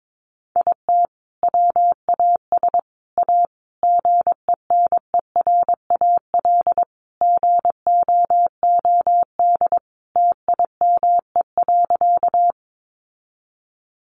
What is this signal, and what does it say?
Morse code, heard as a keyed tone, about 22 wpm, 700 Hz. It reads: IT WAS A GENERAL GOOD TIME.
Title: Sentences from Top 600 Words 22wpm